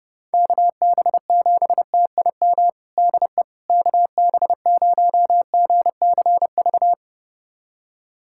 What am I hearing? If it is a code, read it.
Morse code, heard as a keyed tone, about 30 wpm, 700 Hz. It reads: KB7TIM DE KB0GCV
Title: USA Callsign DE Callsign - 6 Characters - 30wpm